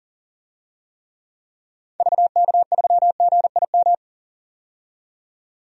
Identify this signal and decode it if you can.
Morse code, heard as a keyed tone, about 40 wpm, 700 Hz. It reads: VK3GIM
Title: DXCC-100 Call Signs - 6 Characters - 40wpm